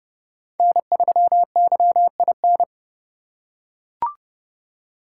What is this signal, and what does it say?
Morse code, heard as a keyed tone, about 28 wpm, 700 Hz. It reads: N3YIN